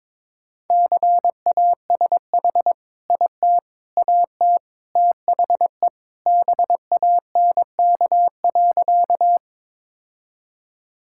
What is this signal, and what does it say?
Morse code, heard as a keyed tone, about 22 wpm, 700 Hz. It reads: CASH IT AT THE BANK.
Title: Sentences from Top 1000 Words - Review Part 5 - 22wpm